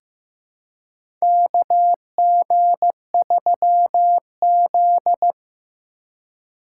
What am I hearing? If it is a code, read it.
Morse code, heard as a keyed tone, about 15 wpm, 700 Hz. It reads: KG3Z
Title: USA Call Signs - 4 Characters - 3x Character Spacing - 15wpm